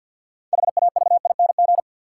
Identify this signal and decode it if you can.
Morse code, heard as a keyed tone, about 53 wpm, 700 Hz. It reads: HAVING
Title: Sets of 5 Words Encore - New Word Drill - 50wpm